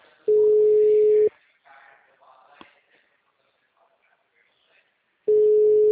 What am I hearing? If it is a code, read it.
Ringback tone (Europe, 425 Hz).
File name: out- -2011- - - .2